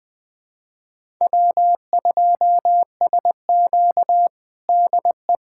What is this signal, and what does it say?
Morse code, heard as a keyed tone, about 21 wpm, 700 Hz. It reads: W2SQ DE
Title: USA Callsign DE Callsign - 4 Characters - 20wpm